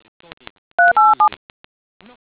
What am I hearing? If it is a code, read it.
DTMF, heard as telephone touch tones, tones of about 160 ms. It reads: 387